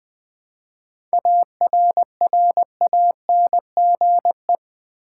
Morse code, heard as a keyed tone, about 20 wpm, 690 Hz. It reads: ARRANGE
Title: 7 Letter Words - First 3 Letters Given - 20wpm